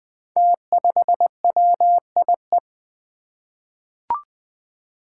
Morse code, heard as a keyed tone, about 20 wpm, 700 Hz. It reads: T5WIE
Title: Five Challenging Letter-Number Combinations 20wpm